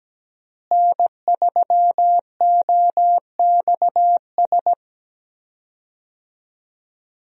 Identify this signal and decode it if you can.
Morse code, heard as a keyed tone, about 17 wpm, 700 Hz. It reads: N3OXS
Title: USA Call Signs - 5 Characters - Repeated 2x - 17wpm